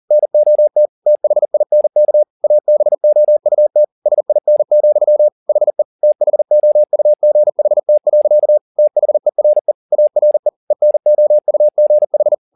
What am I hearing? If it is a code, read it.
Morse code, heard as a keyed tone, about 40 wpm, 600 Hz. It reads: NOT THINK ABOUT SIN, HE THOUGHT. THERE ARE ENOUGH
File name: the old man and the sea 2 40wpm